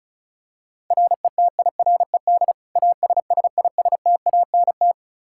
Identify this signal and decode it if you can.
Morse code, heard as a keyed tone, about 36 wpm, 700 Hz. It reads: RETIRED ASSISTANT
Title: Practice QSO - Exchange 3 - More Info - 35wpm